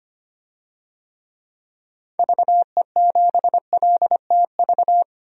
Morse code, heard as a keyed tone, about 25 wpm, 700 Hz. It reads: VE7LTV
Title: DXCC-100 Call Signs - 6 Characters - 25wpm